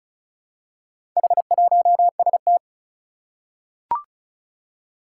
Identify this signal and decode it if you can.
Morse code, heard as a keyed tone, about 36 wpm, 690 Hz. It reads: H1ST
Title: Four Challenging Letter-Number Combinations 35wpm